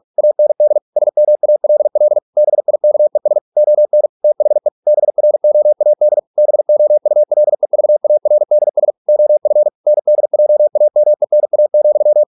Morse code, heard as a keyed tone, about 46 wpm, 600 Hz. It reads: AND SMALL BIKES ON THE BROAD BOULEVARDS OF NDJAMENA,